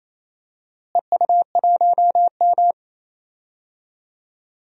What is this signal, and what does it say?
Morse code, heard as a keyed tone, about 28 wpm, 700 Hz. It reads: EU1M
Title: DXCC-100 Call Signs - 4 Characters - Repeated 2x - 28wpm